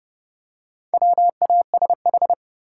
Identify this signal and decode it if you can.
Morse code, heard as a keyed tone, about 30 wpm, 700 Hz. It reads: WASH
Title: Four Letter Words - 30wpm